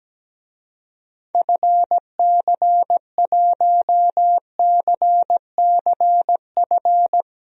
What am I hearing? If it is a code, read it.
Morse code, heard as a keyed tone, about 17 wpm, 700 Hz. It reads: FC1CCF
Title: DXCC-100 Call Signs - 6 Characters - 17wpm